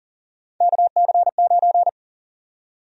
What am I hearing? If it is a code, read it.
Morse code, heard as a keyed tone, about 40 wpm, 700 Hz. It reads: KC9
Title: USA Call Sign Prefixes - Repeated 2x - 40wpm